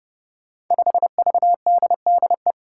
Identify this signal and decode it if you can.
Morse code, heard as a keyed tone, about 30 wpm, 700 Hz. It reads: 5VDDE
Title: Five Challenging Letter-Number Combinations 30wpm